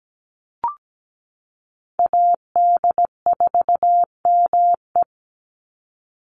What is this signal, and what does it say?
Morse code, heard as a keyed tone, about 17 wpm, 700 Hz. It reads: AD4ME
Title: USA Call Signs 17wpm